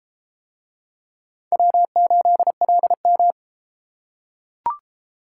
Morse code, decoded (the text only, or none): W8LM